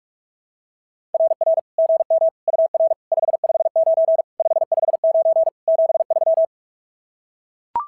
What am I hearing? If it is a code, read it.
Morse code, heard as a keyed tone, about 44 wpm, 660 Hz. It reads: RR GM UR 559 559 73
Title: SOTA QSO Practice 45wpm